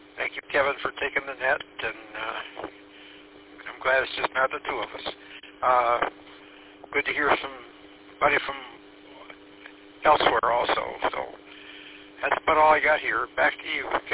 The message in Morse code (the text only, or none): none